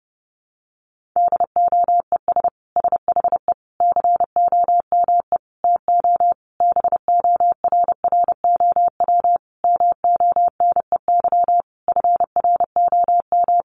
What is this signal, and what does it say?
Morse code, heard as a keyed tone, about 30 wpm, 700 Hz. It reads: DOES SHE COME TO BORROW MONEY FROM
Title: Sentences from Top 1500 Words 30wpm